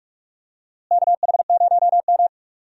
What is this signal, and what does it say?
Morse code, heard as a keyed tone, about 43 wpm, 700 Hz. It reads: KH0M
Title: USA Call Signs - 4 Characters - Repeated 2x - 45wpm